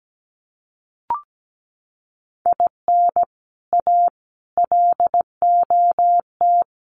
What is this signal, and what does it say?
Morse code, heard as a keyed tone, about 17 wpm, 700 Hz. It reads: IN A LOT
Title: Sets of 5 Words Encore 17wpm